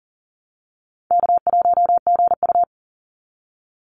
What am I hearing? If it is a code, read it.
Morse code, heard as a keyed tone, about 40 wpm, 700 Hz. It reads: K1GU
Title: Call Signs - 40wpm